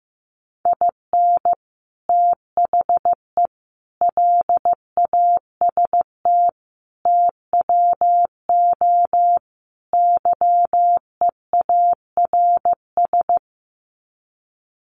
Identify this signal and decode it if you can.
Morse code, heard as a keyed tone, about 15 wpm, 700 Hz. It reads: IN THE LAST TWO YEARS
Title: Sets of 5 Words 15wpm